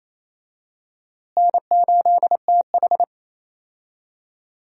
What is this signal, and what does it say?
Morse code, heard as a keyed tone, about 28 wpm, 700 Hz. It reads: N8TH
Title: USA Call Signs - 4 Characters - Repeated 3x - 28wpm